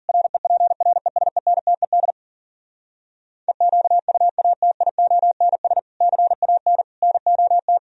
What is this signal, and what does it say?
Morse code, heard as a keyed tone, about 59 wpm, spaced as for 51 wpm, 700 Hz. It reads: REPRESENTED EQUATIONS CAN NOT